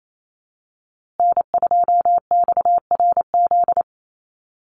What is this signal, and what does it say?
Morse code, heard as a keyed tone, about 28 wpm, 700 Hz. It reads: N2XRZ
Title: USA Call Signs Repeated 3x 28wpm